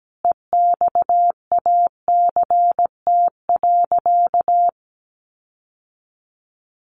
Morse code, heard as a keyed tone, about 17 wpm, 700 Hz. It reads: EXACT.
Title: Sentences from Top 1300 Words 17wpm